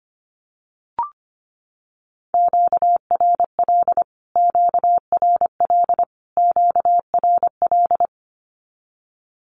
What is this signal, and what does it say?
Morse code, heard as a keyed tone, about 25 wpm, 700 Hz. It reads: QRL QRL QRL
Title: Q-Codes Repeated 3x 25wpm